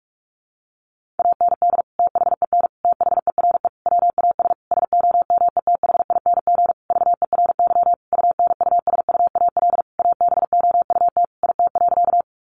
Morse code, heard as a keyed tone, about 44 wpm, 700 Hz. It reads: AND THEN THERE WAS SOMETHING VERY UNUSUAL ABOUT IT.
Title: Sentences from Top 1400 Words 45wpm